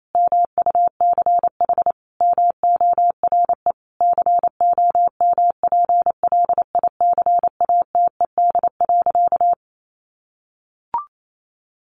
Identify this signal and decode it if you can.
Morse code, heard as a keyed tone, about 28 wpm, 700 Hz. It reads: MUCH MORE COMPLICATED.
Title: Sentences from Top 1200 Words 28wpm